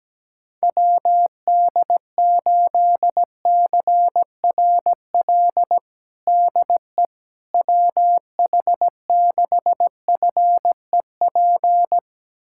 Morse code, heard as a keyed tone, about 17 wpm, 700 Hz. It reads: WD8CRL DE WH6FEP
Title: USA Callsign DE Callsign - 6 Characters - 17wpm